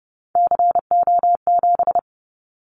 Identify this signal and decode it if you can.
Morse code, heard as a keyed tone, about 30 wpm, 700 Hz. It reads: CO7
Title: Callsign Prefixes for Top 100 DXCC - 30wpm